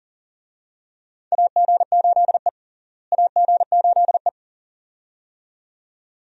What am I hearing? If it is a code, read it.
Morse code, heard as a keyed tone, about 40 wpm, 700 Hz. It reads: AG8E AG8E
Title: USA Call Signs - 4 Characters - Repeated 2x - 40wpm